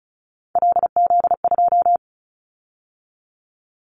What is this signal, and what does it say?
Morse code, heard as a keyed tone, about 34 wpm, 700 Hz. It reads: LZ2
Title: Callsign Prefixes for Top 100 DXCC - 35wpm